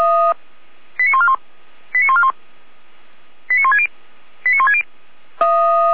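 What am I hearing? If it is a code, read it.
Five-tone selective call (ZVEI), heard as quick runs of five tones, tones of about 70 ms. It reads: 89141 89141 89160 89160
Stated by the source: ZVEI five-tone selective call received on an SDR, from the Feuerwehr Orlamünde audio server.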